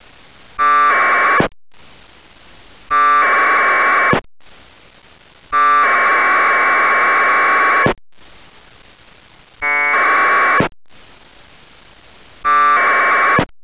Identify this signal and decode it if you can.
Packet radio (AX.25, as in APRS).